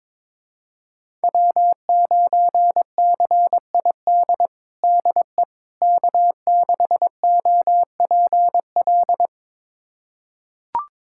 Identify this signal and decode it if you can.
Morse code, heard as a keyed tone, about 22 wpm, 700 Hz. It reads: W9CID DE K6OPL
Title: USA Callsign DE Callsign - 5 Characters - 22wpm